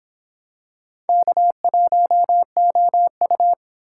Morse code, heard as a keyed tone, about 26 wpm, 700 Hz. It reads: K1OU